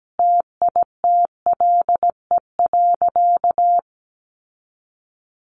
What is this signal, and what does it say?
Morse code, heard as a keyed tone, about 17 wpm, 700 Hz. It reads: TITLE.
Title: Sentences from Top 1200 Words 17wpm